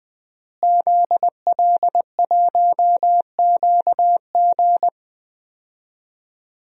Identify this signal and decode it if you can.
Morse code, heard as a keyed tone, about 20 wpm, 700 Hz. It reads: ZL1QG
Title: DXCC-100 Call Signs - 5 Characters - Repeated 2x - 20wpm